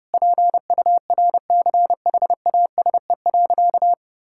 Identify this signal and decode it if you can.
Morse code, heard as a keyed tone, about 31 wpm, 700 Hz. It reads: PURCHASE.
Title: Sentences from Top 1000 Words - Review Part 5 - 30wpm